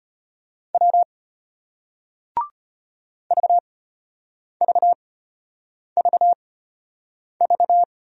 Morse code, heard as a keyed tone, about 33 wpm, spaced as for 8 wpm, 700 Hz. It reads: W VVVV